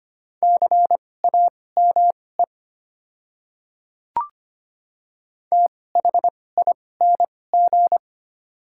Morse code, heard as a keyed tone, about 25 wpm, spaced as for 18 wpm, 690 Hz. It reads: CAME THING
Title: 2x Character Spacing - Top 200 Words - 25wpm